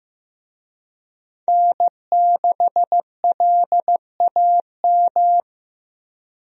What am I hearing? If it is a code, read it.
Morse code, heard as a keyed tone, about 15 wpm, 700 Hz. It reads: N6LAM
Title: USA Call Signs - 5 Characters - Repeated 3x - 15wpm